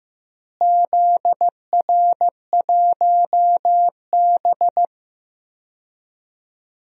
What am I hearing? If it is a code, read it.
Morse code, heard as a keyed tone, about 16 wpm, 700 Hz. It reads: ZR1B
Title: DXCC-100 Call Signs - 4 Characters - Repeated 2x - 15wpm